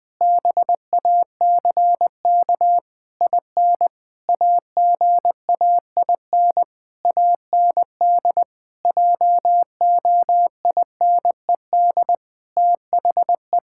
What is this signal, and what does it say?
Morse code, heard as a keyed tone, about 21 wpm, 700 Hz. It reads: BACK IN AGAIN AND JOINED THE
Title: Pangram Challenge v2 - 20wpm